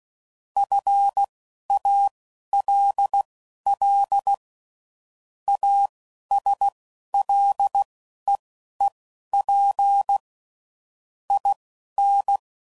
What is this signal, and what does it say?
Morse code, heard as a keyed tone, about 16 wpm, spaced as for 11 wpm, 800 Hz. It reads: FALL ASLEEP IN